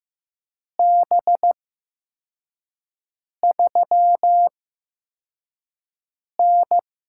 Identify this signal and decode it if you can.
Morse code, heard as a keyed tone, about 15 wpm, spaced as for 4 wpm, 700 Hz. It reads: B3N
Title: USA Call Signs - 4 Characters - 8x Character Spacing - 15wpm